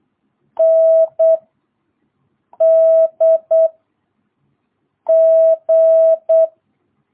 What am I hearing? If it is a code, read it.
Morse code, heard as a keyed tone, about 8 wpm, 650 Hz. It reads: N D G
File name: jh8cbh / 2022cw